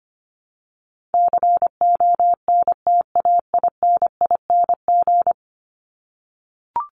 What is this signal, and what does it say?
Morse code, heard as a keyed tone, about 25 wpm, 700 Hz. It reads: CONTAINING
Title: ING Suffix - Top 100 Words with Base Words 25wpm